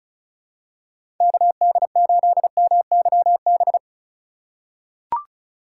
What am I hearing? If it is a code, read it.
Morse code, heard as a keyed tone, about 36 wpm, 700 Hz. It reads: KD8MYB
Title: USA Call Signs - 3x Character Spacing - 35wpm